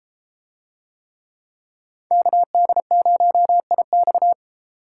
Morse code, heard as a keyed tone, about 31 wpm, 700 Hz. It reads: KD0IX